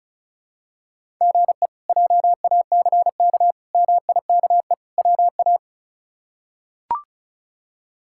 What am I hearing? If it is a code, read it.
Morse code, heard as a keyed tone, about 36 wpm, 700 Hz. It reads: GE JACK MIKE WA
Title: K1USN SST QSO Practice - 35wpm